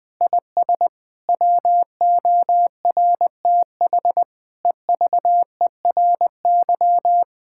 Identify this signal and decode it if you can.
Morse code, heard as a keyed tone, about 21 wpm, 700 Hz. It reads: IS WORTH EVERY
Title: Difficult Sentences with Hints - 20wpm